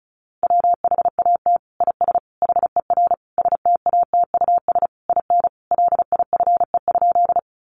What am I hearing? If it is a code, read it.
Morse code, heard as a keyed tone, about 34 wpm, 700 Hz. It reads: WHAT IS HER STATUS IN LIFE?
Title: Sentences from Top 1400 Words 35wpm